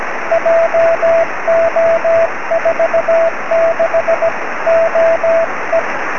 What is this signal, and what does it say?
Morse code, heard as a keyed tone, about 17 wpm, 670 Hz, with radio noise. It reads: JO46OE
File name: sm7lcb / 20-oz1ctz 23cm